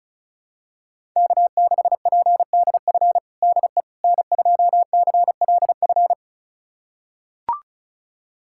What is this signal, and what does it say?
Morse code, heard as a keyed tone, about 34 wpm, 690 Hz. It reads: K6PDF DE N2CLF
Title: USA Callsign DE Callsign - 5 Characters - 35wpm